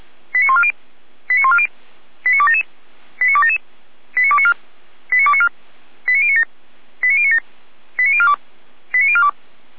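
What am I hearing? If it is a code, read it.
Five-tone selective call (ZVEI), heard as quick runs of five tones, tones of about 80 ms. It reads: 89140 89140 89277 89277 89294 89294 89097 89097 89042 89042
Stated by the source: Five-tone ZVEI selective call 89187, received by SDR.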